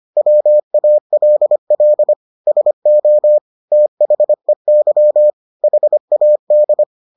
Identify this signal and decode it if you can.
Morse code, heard as a keyed tone, about 25 wpm, 600 Hz. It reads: WALL SO THEY HAD